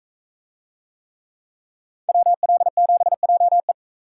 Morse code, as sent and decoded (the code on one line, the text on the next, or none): .-- .-.. --... .--- .
WL7JE